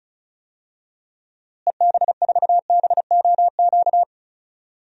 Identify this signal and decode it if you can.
Morse code, heard as a keyed tone, about 34 wpm, 700 Hz. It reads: EB4BOQ
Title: DXCC-100 Call Signs - 6 Characters - Repeated 3x - 35wpm